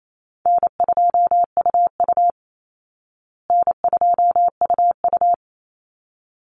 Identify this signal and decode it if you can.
Morse code, heard as a keyed tone, about 28 wpm, 700 Hz. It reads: N2UU N2UU
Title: Call Signs Repeated 3x - 28wpm